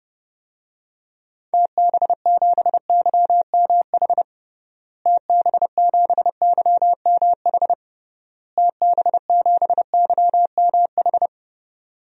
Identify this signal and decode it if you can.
Morse code, heard as a keyed tone, about 31 wpm, 700 Hz. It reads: TB7YMH TB7YMH TB7YMH
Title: DXCC-100 Call Signs - 6 Characters - Repeated 3x - 30wpm